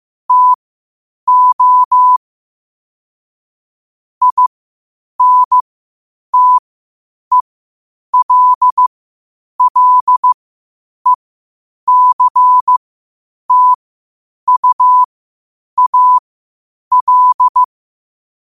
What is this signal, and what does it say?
Morse code, heard as a keyed tone, about 15 wpm, spaced as for 8 wpm, 1000 Hz. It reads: TO INTELLECTUAL
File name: qotdfiles / 8wpm